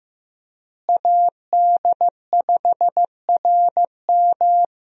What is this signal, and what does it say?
Morse code, heard as a keyed tone, about 15 wpm, 700 Hz. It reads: AD5RM
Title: USA Call Signs - 5 Characters - Repeated 2x - 15wpm